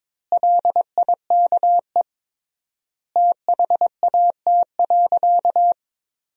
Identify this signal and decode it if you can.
Morse code, heard as a keyed tone, about 21 wpm, 700 Hz. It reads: LIKE THAT.
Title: 3x Word-Spacing - Sentences from Top 500 Words - 22wpm